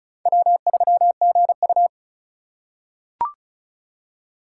Morse code, heard as a keyed tone, about 34 wpm, 700 Hz. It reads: W3GU
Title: USA Call Signs - 4 Characters - 4x Character Spacing - 35wpm